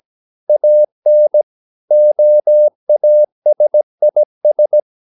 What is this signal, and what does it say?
Morse code, heard as a keyed tone, about 17 wpm, 600 Hz. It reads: AN OASIS
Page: Romanen Never af Ken Follet på engelsk.